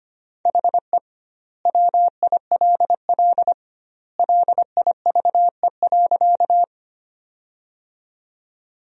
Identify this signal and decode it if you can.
Morse code, heard as a keyed tone, about 25 wpm, 700 Hz. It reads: HE WILL LIVE.
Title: 2x Word-Spacing - Sentences from Top 200 Words - 25wpm